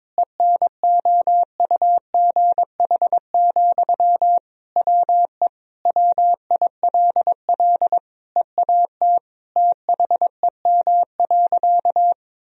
Morse code, heard as a keyed tone, about 22 wpm, 700 Hz. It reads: ENOUGH, WE WILL EAT THEM.
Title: Sentences from Top 1000 Words - Review Part 4 - 22wpm